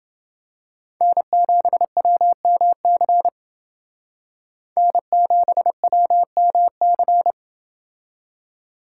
Morse code, heard as a keyed tone, about 31 wpm, 700 Hz. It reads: N7WMC N7WMC